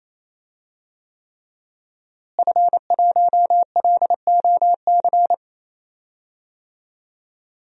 Morse code, heard as a keyed tone, about 28 wpm, 700 Hz. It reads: F1LOC
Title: DXCC-100 Call Signs - 5 Characters - Repeated 3x - 28wpm